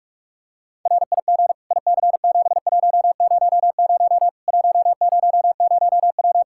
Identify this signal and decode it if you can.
Morse code, heard as a keyed tone, about 41 wpm, 700 Hz. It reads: RIG IC7100 100W
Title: Practice QSO - Exchange 2 - Info - 45wpm